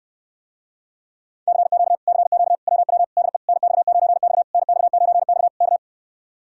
Morse code, heard as a keyed tone, about 69 wpm, 700 Hz. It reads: CQ CQ WW DE NX9X NX9X K